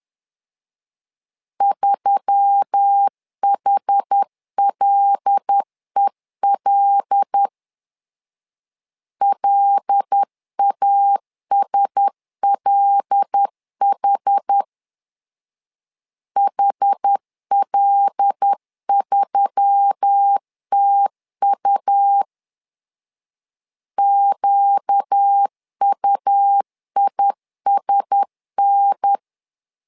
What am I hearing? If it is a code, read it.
Morse code, heard as a keyed tone, about 10 wpm, 800 Hz. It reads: 3HLEL LASLH HL3TU QUISN
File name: LEZIONI / 5 LBVC / 5 L 10wpm 2259